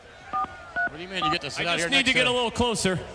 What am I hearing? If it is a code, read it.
DTMF, heard as telephone touch tones, tones of about 120 ms, with speech. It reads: 434